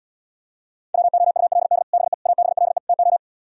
Morse code, heard as a keyed tone, about 77 wpm, 700 Hz. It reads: WOULD BE ABLE TO